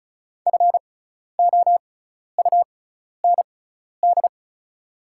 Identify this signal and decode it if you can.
Morse code, heard as a keyed tone, about 36 wpm, spaced as for 12 wpm, 700 Hz. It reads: FOUND
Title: 6x Character Spacing - Top 300 Words - 35wpm